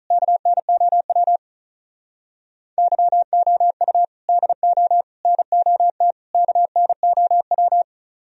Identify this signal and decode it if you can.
Morse code, heard as a keyed tone, about 45 wpm, 700 Hz. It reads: KNOW YOU DO NOT KNOW